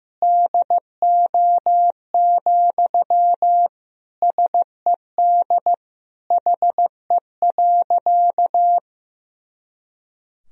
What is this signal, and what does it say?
Morse code, heard as a keyed tone, about 15 wpm, 700 Hz. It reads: DO, SED HE.
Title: Sentences from Top 100 Words and Abbreviations 15wpm